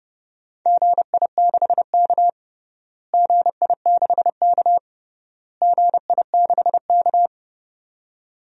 Morse code, heard as a keyed tone, about 30 wpm, 700 Hz. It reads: GI6K GI6K GI6K
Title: DXCC-100 Call Signs - 4 Characters - Repeated 3x - 30wpm